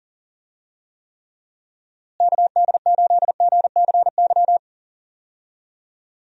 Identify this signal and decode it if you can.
Morse code, heard as a keyed tone, about 40 wpm, 700 Hz. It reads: KD8GCY
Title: USA Call Signs - 6 Characters - Repeated 3x - 40wpm